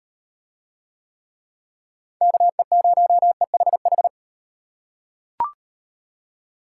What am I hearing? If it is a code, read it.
Morse code, heard as a keyed tone, about 34 wpm, 700 Hz. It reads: KE0EHH